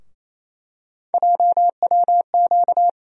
Morse code, heard as a keyed tone, about 28 wpm, 700 Hz. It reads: JWQ
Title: USA Call Sign Postfixes - 3 Letters 28wpm